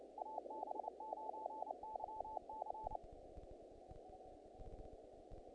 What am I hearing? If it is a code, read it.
Morse code, heard as a keyed tone, about 28 wpm, 870 Hz, with noise. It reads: AB9YC
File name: contests / naqp-cw-k5tr-op-k5pi- -1900